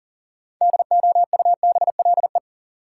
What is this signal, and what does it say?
Morse code, heard as a keyed tone, about 37 wpm, 700 Hz. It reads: DOUBLE
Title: Sentences from Top 1400 Words - New Word Drill Alt - 40wpm